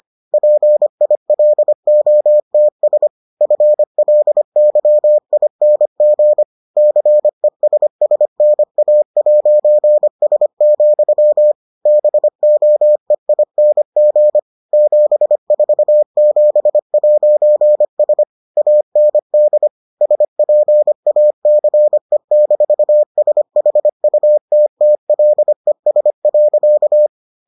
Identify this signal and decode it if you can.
Morse code, heard as a keyed tone, about 25 wpm, 600 Hz. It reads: PILOTS FLYING CESSNA'S, BOEING 747'S AND SPACE-SHUTTLES.